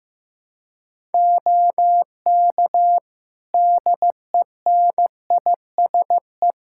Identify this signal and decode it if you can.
Morse code, heard as a keyed tone, about 15 wpm, 700 Hz. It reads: OK DENISE
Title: Practice QSO - Exchange 4 - Ending - 15wpm